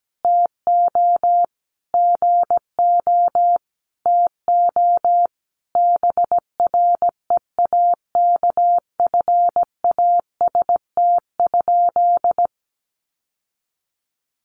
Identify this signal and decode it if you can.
Morse code, heard as a keyed tone, about 17 wpm, 700 Hz. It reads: TO GO TO BREAKFAST?
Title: Sentences from Top 1500 Words 17wpm